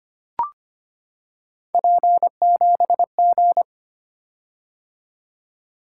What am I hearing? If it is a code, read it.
Morse code, heard as a keyed tone, about 25 wpm, 700 Hz. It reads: P7G
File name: Three Letter-Number Combinations - 25wpm